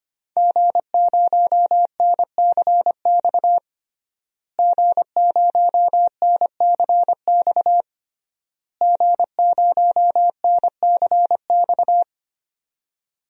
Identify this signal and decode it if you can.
Morse code, heard as a keyed tone, about 25 wpm, 700 Hz. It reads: G0NCX G0NCX G0NCX
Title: DXCC-100 Call Signs - 5 Characters - Repeated 3x - 25wpm